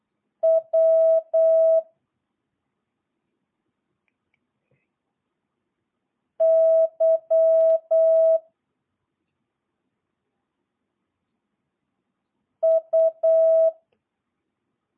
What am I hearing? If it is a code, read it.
Morse code, heard as a keyed tone, about 8 wpm, 650 Hz. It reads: W Y U